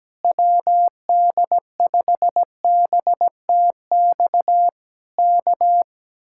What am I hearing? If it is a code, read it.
Morse code, heard as a keyed tone, about 17 wpm, 700 Hz. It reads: WD5BTX K
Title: Speed-Racing Common QSO Phrases Encore - 17wpm